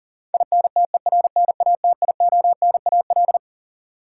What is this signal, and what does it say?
Morse code, heard as a keyed tone, about 40 wpm, 700 Hz. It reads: INTERNATIONAL
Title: Sentences from Top 1000 Words - Review Part 4 - New Word Drill - 40wpm